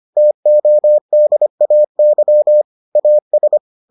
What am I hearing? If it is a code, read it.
Morse code, heard as a keyed tone, about 25 wpm, 600 Hz. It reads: TODAY AS